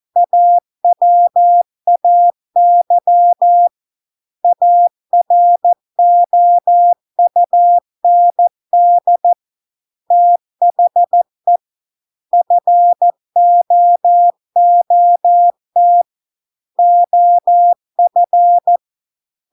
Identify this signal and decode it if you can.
Morse code, heard as a keyed tone, about 14 wpm, 700 Hz. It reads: AWAY AROUND THE FOOT OF